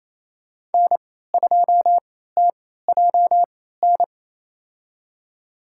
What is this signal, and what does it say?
Morse code, heard as a keyed tone, about 28 wpm, spaced as for 16 wpm, 700 Hz. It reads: N2TJN
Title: USA Call Signs - 3x Character Spacing - 28wpm